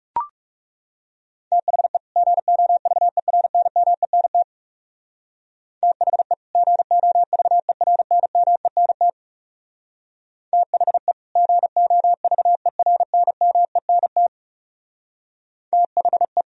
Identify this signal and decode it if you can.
Morse code, heard as a keyed tone, about 38 wpm, 700 Hz. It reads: THE GOVERNMENT THE GOVERNMENT THE GOVERNMENT THE
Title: Speed-Racing - Sets of 2 Words 30wpm